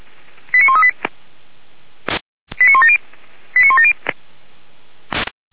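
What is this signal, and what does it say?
Five-tone selective call (ZVEI), heard as quick runs of five tones, tones of about 70 ms. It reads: 89138 89170 89170
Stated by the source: ZVEI five-tone call 89136 received by SDR, from the Feuerwehr Orlamünde fire brigade audio archive.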